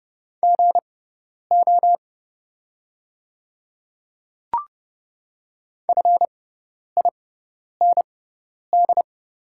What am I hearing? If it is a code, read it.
Morse code, heard as a keyed tone, about 29 wpm, spaced as for 10 wpm, 700 Hz. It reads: GO FIND
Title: 6x Character Spacing - Top 100 Words - 30wpm